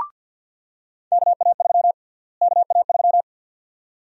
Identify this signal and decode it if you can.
Morse code, heard as a keyed tone, about 51 wpm, 700 Hz. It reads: KA3 KA3
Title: USA Call Sign Prefixes - Repeated 2x - 50wpm